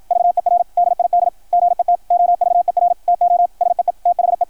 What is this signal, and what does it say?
Morse code, heard as a keyed tone, about 53 wpm, 660 Hz, with radio noise. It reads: VER DID GET OVER TO SEE THE
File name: archived / CRSnet